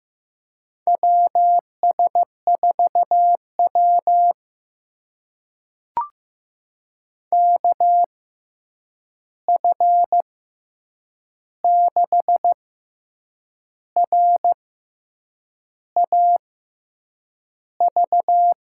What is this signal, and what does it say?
Morse code, heard as a keyed tone, about 15 wpm, spaced as for 7 wpm, 700 Hz. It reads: WS4W KF6RAV
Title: USA Call Signs - 6x Character Spacing - 15wpm